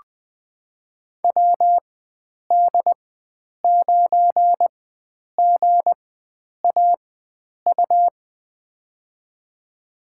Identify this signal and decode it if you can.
Morse code, heard as a keyed tone, about 20 wpm, spaced as for 9 wpm, 700 Hz. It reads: WD9GAU
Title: USA Call Signs - 6 Characters - 4x Character Spacing - 20wpm